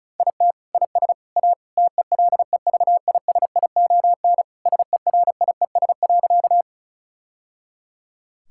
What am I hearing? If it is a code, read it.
Morse code, heard as a keyed tone, about 34 wpm, 700 Hz. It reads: IT IS A TELEVISION SERIES.
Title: Sentences from Top 1000 Words - Review Part 5 - 35wpm